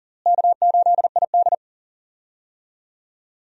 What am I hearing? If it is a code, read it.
Morse code, heard as a keyed tone, about 40 wpm, 700 Hz. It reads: K8ID
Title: USA Call Signs - 4 Characters - Repeated 2x - 40wpm